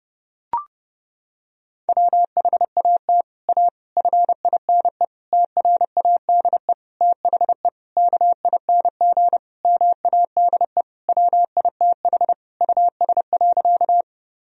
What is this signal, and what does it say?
Morse code, heard as a keyed tone, about 30 wpm, 700 Hz. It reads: WHAT A FINE TRADE THE KING MADE WITH US.
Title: Sentences from Top 1000 Words - Review Part 4 - 30wpm